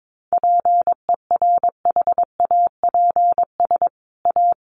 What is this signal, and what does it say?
Morse code, heard as a keyed tone, about 22 wpm, 700 Hz. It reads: PERHAPS A
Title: Sentences from Top 1500 Words 22wpm